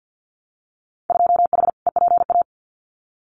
Morse code, heard as a keyed tone, about 50 wpm, 700 Hz. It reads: 2H EPA
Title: Winter Field Day Exchanges Encore 50wpm